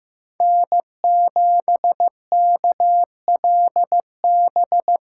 Morse code, heard as a keyed tone, about 18 wpm, 700 Hz. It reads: N7KLB
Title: Speed Racing - USA 1x3 Call Signs 15wpm